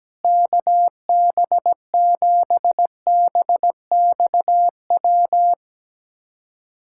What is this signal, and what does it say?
Morse code, heard as a keyed tone, about 17 wpm, 700 Hz. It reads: KB7BXW
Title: USA Callsign DE Callsign 17wpm